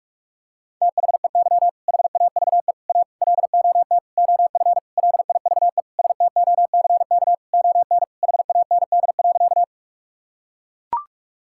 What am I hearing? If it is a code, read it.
Morse code, heard as a keyed tone, about 42 wpm, 700 Hz. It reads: THEY HAVE A LOT OF LIVESTOCK ON HAND.
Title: Sentences from Top 1000 Words - Review Part 5 - 45wpm